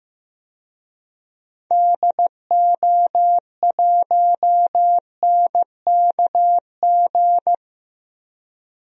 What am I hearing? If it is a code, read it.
Morse code, heard as a keyed tone, about 15 wpm, 700 Hz. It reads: DO1NKG
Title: DXCC-100 Call Signs - 6 Characters - 15wpm